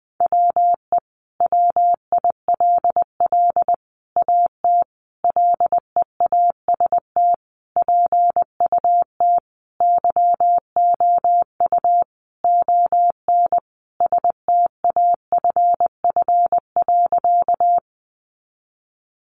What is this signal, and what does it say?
Morse code, heard as a keyed tone, about 20 wpm, 700 Hz. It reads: WE WILL AT LEAST PUT YOU ON STAFF.
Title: Sentences from Top 1200 Words 20wpm